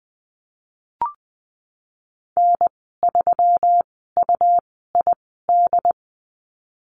Morse code, heard as a keyed tone, about 20 wpm, spaced as for 15 wpm, 700 Hz. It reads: N3UID
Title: 122 Lesson 20g - Number 3 - Call Signs 20wpm